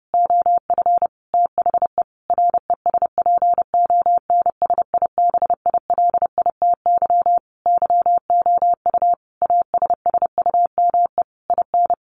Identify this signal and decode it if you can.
Morse code, heard as a keyed tone, about 30 wpm, 700 Hz. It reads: OF THE RESPONSIBILITY YOU ASSUME IN
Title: Sentences from Top 1500 Words 30wpm